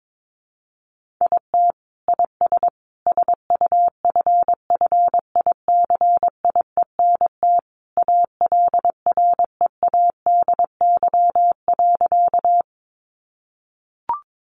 Morse code, heard as a keyed tone, about 22 wpm, 700 Hz. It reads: IT IS SUFFICIENT ALREADY.
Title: Sentences from Top 1400 Words 22wpm